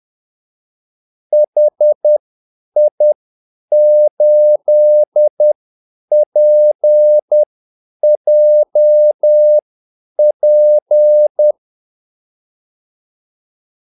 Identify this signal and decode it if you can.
Morse code, heard as a keyed tone, about 10 wpm, spaced as for 8 wpm, 600 Hz. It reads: HI8PJP
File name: Callsigns-10-8wpm